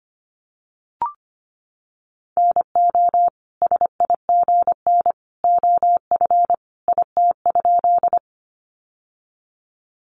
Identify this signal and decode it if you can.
Morse code, heard as a keyed tone, about 25 wpm, 700 Hz. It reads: NO SIGN OF IT?
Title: Sentences from Top 1000 Words 25wpm